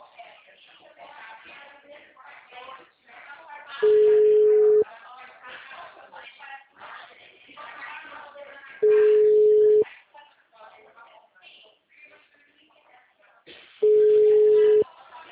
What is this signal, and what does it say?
Ringback tone (Europe, 425 Hz), with speech.
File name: out- -2007- - - .2654